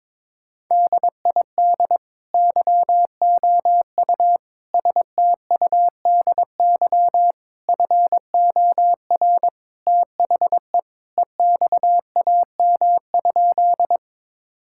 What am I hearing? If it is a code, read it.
Morse code, heard as a keyed tone, about 21 wpm, 700 Hz. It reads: DID YOU STUDY FOR THE EXAM?
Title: Sentences from Top 1000 Words - Review Part 5 - 22wpm